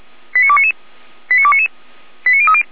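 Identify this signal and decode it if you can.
Five-tone selective call (ZVEI), heard as quick runs of five tones, tones of about 70 ms. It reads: 89299 89299 89030
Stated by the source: Five-tone ZVEI call received by SDR, feuerwehr-orlamuende.org (fire brigade Orlamünde).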